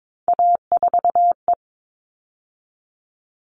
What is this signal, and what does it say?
Morse code, heard as a keyed tone, about 22 wpm, 700 Hz. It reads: A4E
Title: All Cut Numbers - 3-Digit 22wpm